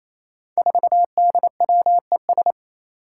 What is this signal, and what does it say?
Morse code, heard as a keyed tone, about 27 wpm, 700 Hz. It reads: 4DWES
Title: Five Challenging Letter-Number Combinations 28wpm